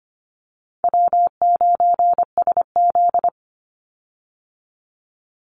Morse code, heard as a keyed tone, about 25 wpm, 700 Hz. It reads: W9SZ
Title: Call Signs Repeated 3x - 25wpm